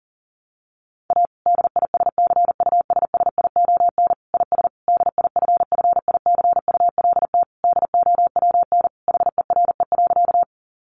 Morse code, heard as a keyed tone, about 40 wpm, 700 Hz. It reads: A DISCUSSION IS DIFFICULT DOWN HERE.
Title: Sentences from Top 1100 Words 40wpm